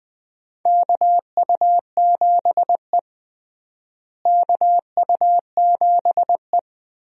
Morse code, heard as a keyed tone, about 20 wpm, 700 Hz. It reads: KU7E KU7E
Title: USA Call Signs - 4 Characters - Repeated 2x - 20wpm